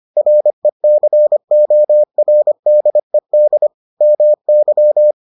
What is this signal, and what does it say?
Morse code, heard as a keyed tone, about 25 wpm, 600 Hz. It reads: RECORDED MY